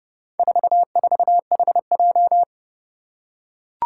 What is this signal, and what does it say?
Morse code, heard as a keyed tone, about 30 wpm, 700 Hz. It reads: 44HJ
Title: Four Challenging Letter-Number Combinations 30wpm